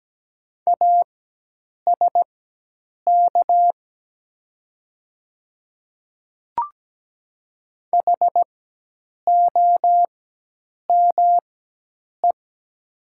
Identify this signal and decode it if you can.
Morse code, heard as a keyed tone, about 17 wpm, spaced as for 8 wpm, 700 Hz. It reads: ASK HOME
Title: 4x Character Spacing - Top 300 Words - 17wpm